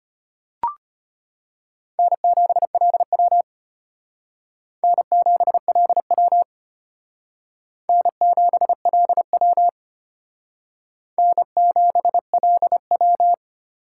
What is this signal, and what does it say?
Morse code, heard as a keyed tone, about 32 wpm, 700 Hz. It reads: N7LW N7LW N7LW N7LW